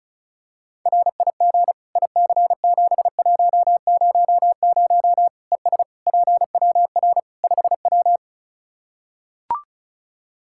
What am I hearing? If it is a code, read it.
Morse code, heard as a keyed tone, about 36 wpm, 700 Hz. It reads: RIG IC7100 ES PWR 5W
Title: Practice QSO - Exchange 2 - Info - 35wpm